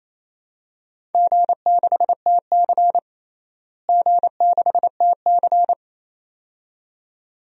Morse code, heard as a keyed tone, about 27 wpm, 700 Hz. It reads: G6TC G6TC
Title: DXCC-100 Call Signs - 4 Characters - Repeated 2x - 28wpm